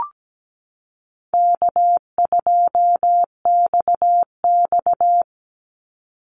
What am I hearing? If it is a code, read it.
Morse code, heard as a keyed tone, about 17 wpm, 700 Hz. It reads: K2XX
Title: Easy Call Signs - 17wpm